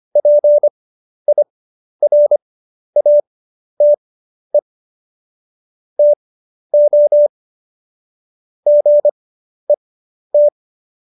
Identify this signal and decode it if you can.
Morse code, heard as a keyed tone, about 25 wpm, spaced as for 11 wpm, 590 Hz. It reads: PIRATE TO GET
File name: Jokes-25-char-10-eff